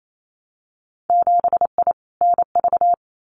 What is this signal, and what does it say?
Morse code, heard as a keyed tone, about 28 wpm, 690 Hz. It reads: 7I NV
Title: Winter Field Day Exchanges Encore 28wpm